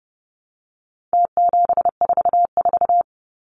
Morse code, heard as a keyed tone, about 30 wpm, 700 Hz. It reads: T744
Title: Numbers - Contest Style - 4-Digit 30wpm